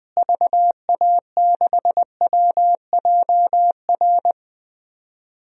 Morse code, heard as a keyed tone, about 21 wpm, 700 Hz. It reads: VA6WJR
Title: DXCC-100 Call Signs - 6 Characters - Repeated 2x - 20wpm